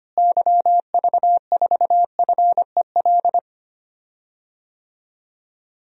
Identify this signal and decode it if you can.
Morse code, heard as a keyed tone, about 25 wpm, 700 Hz. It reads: YV4FEL
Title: DXCC-100 Call Signs - 6 Characters - Repeated 3x - 25wpm